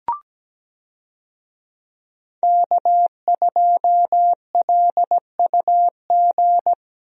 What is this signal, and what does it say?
Morse code, heard as a keyed tone, about 17 wpm, 700 Hz. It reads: K2LUG
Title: USA Call Signs - 5 Characters - Repeated 3x - 17wpm